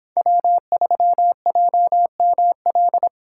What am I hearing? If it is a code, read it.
Morse code, heard as a keyed tone, about 23 wpm, 700 Hz. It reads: W3JML